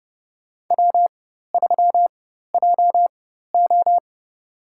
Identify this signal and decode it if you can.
Morse code, heard as a keyed tone, about 30 wpm, spaced as for 14 wpm, 700 Hz. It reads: W3JO
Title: USA Call Signs - 4 Characters - 4x Character Spacing - 30wpm